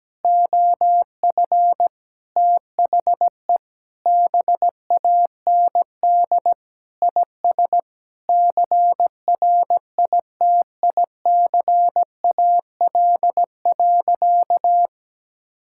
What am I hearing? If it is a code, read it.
Morse code, heard as a keyed tone, about 17 wpm, 700 Hz. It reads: OF THE BAND IS CRITICAL.
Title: Sentences from Top 1600 Words 17wpm